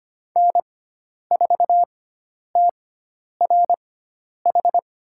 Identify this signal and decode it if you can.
Morse code, heard as a keyed tone, about 25 wpm, spaced as for 10 wpm, 700 Hz. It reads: N4TRH
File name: Lesson 16g - Letter c - Call Signs 25wpm